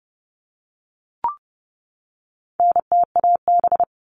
Morse code, heard as a keyed tone, about 30 wpm, 700 Hz. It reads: NTAB
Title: All Cut Numbers - 4-Digit 30wpm